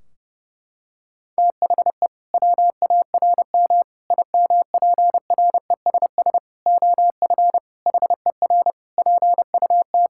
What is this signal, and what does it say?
Morse code, heard as a keyed tone, about 30 wpm, 700 Hz. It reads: THE WARM IMPRESS OF HER PUT
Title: Sentences from Top 1600 Words 30wpm